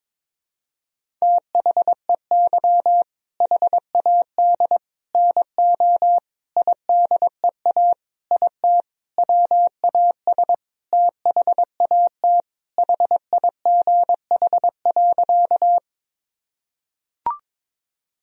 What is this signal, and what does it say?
Morse code, heard as a keyed tone, about 22 wpm, 700 Hz. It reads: THEY HAD NO IDEA IT WAS THAT HIGH.
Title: Sentences from Top 1000 Words - Review Part 1 - 22wpm